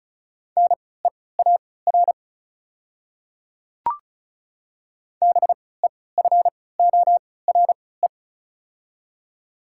Morse code, heard as a keyed tone, about 34 wpm, spaced as for 20 wpm, 690 Hz. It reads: NEAR BEFORE
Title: 3x Character Spacing - Top 300 Words - 35wpm